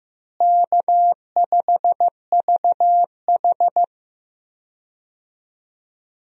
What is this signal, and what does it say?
Morse code, heard as a keyed tone, about 15 wpm, 700 Hz. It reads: K5VH
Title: USA Call Signs - 4 Characters - Repeated 3x - 15wpm